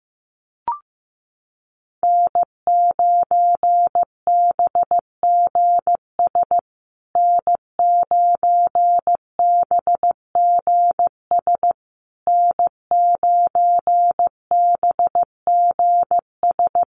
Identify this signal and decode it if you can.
Morse code, heard as a keyed tone, about 15 wpm, 700 Hz. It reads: N9BGS N9BGS N9BGS
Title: USA Call Signs Repeated 3x 15wpm